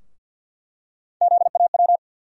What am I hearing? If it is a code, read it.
Morse code, heard as a keyed tone, about 50 wpm, 700 Hz. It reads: ZAW
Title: USA Call Sign Postfixes - 3 Letters 50wpm